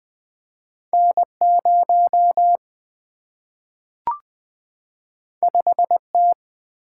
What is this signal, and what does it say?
Morse code, heard as a keyed tone, about 20 wpm, 700 Hz. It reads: N 0 5 T
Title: Single-Letter Single-Number 20wpm